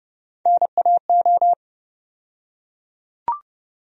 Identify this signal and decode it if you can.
Morse code, heard as a keyed tone, about 30 wpm, 700 Hz. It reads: NAO
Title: USA Call Sign Postfixes - 3 Letters 30wpm